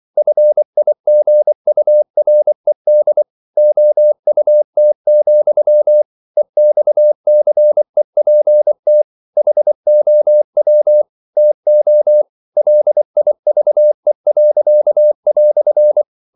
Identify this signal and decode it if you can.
Morse code, heard as a keyed tone, about 24 wpm, 600 Hz. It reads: FIGURED OUT, EXCEPT HOW TO LIVE."
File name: CW / Practice / EnglishQuotations0324